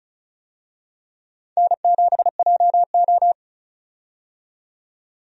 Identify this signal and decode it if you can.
Morse code, heard as a keyed tone, about 36 wpm, 700 Hz. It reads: N7JO
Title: USA Call Signs - 4 Characters - 35wpm